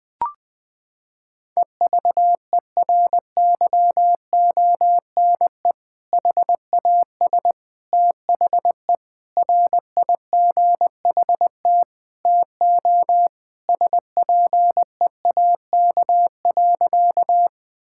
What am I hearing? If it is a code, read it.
Morse code, heard as a keyed tone, about 20 wpm, 700 Hz. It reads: EVERYONE HAS THE RIGHT TO SPEAK.
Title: Sentences with 3-Word Phrases - 20wpm